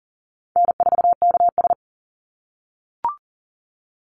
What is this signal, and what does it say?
Morse code, heard as a keyed tone, about 40 wpm, 700 Hz. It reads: N4KS
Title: Easy Call Signs - 40wpm